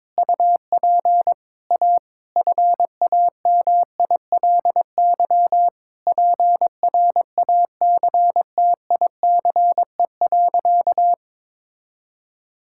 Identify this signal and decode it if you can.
Morse code, heard as a keyed tone, about 22 wpm, 700 Hz. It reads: UP A FAMILY PRACTICE.
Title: Sentences from Top 1000 Words - Review Part 5 - 22wpm